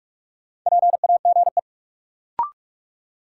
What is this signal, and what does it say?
Morse code, heard as a keyed tone, about 46 wpm, 700 Hz. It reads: PAGE
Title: Sentences from Top 1000 Words - Review Part 1 - New Word Drill - 45wpm